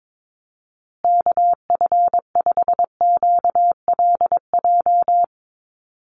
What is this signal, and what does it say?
Morse code, heard as a keyed tone, about 22 wpm, 700 Hz. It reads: KF5QLJ
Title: USA Call Signs Repeated 3x 22wpm